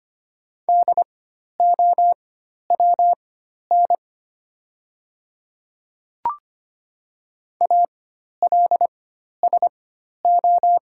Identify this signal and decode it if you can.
Morse code, heard as a keyed tone, about 25 wpm, spaced as for 12 wpm, 690 Hz. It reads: DOWN ALSO
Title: 4x Character Spacing - Top 200 Words - 25wpm